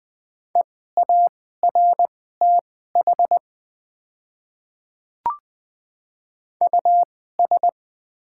Morse code, heard as a keyed tone, about 20 wpm, spaced as for 14 wpm, 700 Hz. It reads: EARTH US
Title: 2x Character Spacing - Top 200 Words - 20wpm